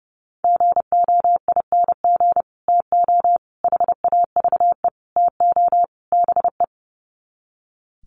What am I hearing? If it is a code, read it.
Morse code, heard as a keyed tone, about 30 wpm, 700 Hz. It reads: GOING TO HAVE TO BE
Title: Sets of 5 Words 30wpm